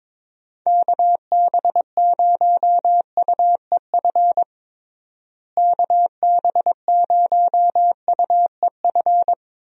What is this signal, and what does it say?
Morse code, heard as a keyed tone, about 22 wpm, 700 Hz. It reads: KB0UEF KB0UEF
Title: USA Call Signs - 6 Characters - Repeated 2x - 22wpm